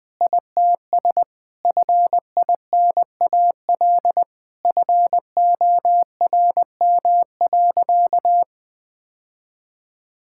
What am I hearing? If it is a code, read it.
Morse code, heard as a keyed tone, about 21 wpm, 700 Hz. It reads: ITS FINAL FORM.
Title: Sentences with 7-Word Phrases - Spoken Beginning - 20wpm